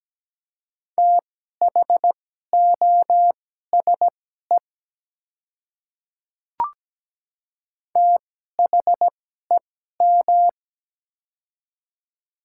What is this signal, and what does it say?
Morse code, heard as a keyed tone, about 17 wpm, spaced as for 12 wpm, 700 Hz. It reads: THOSE THEM
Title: 2x Character Spacing - Top 300 Words - 17wpm